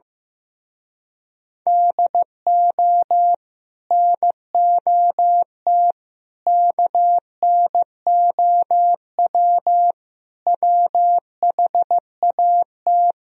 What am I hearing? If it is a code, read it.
Morse code, heard as a keyed tone, about 15 wpm, 700 Hz. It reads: DO NOT KNOW WHAT
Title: Speed-Racing - Sets of 5 Words 15wpm